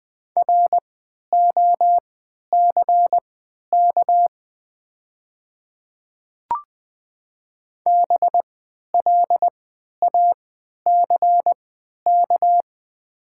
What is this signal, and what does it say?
Morse code, heard as a keyed tone, about 20 wpm, spaced as for 11 wpm, 700 Hz. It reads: ROCK BLACK
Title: 3x Character Spacing - Top 500 Words - 20wpm